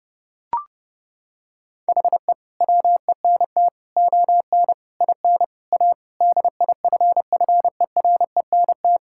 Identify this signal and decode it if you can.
Morse code, heard as a keyed tone, about 30 wpm, 700 Hz. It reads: HE WENT ON IN A DIFFERENT
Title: Sentences from Top 1600 Words 30wpm